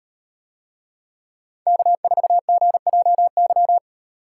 Morse code, heard as a keyed tone, about 36 wpm, 700 Hz. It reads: K4GJY